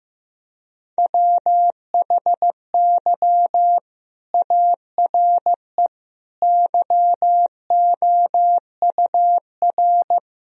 Morse code, heard as a keyed tone, about 15 wpm, 700 Hz. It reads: WHY ARE YOUR
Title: Sentences from Top 1000 Words - Review Part 2 - 15wpm